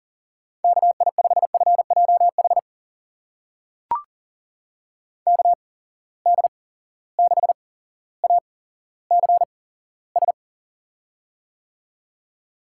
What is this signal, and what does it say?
Morse code, heard as a keyed tone, about 40 wpm, spaced as for 17 wpm, 700 Hz. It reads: KI5FJH KD6ACS
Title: USA Call Signs - 6 Characters - 8x Character Spacing - 40wpm